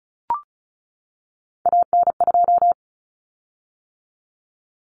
Morse code, heard as a keyed tone, about 34 wpm, 700 Hz. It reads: AN2
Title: Common Cut Numbers - 3-Digit 35wpm